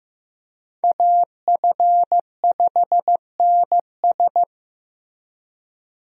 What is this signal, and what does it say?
Morse code, heard as a keyed tone, about 16 wpm, 700 Hz. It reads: AF5NS
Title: USA Call Signs - 5 Characters - Repeated 2x - 15wpm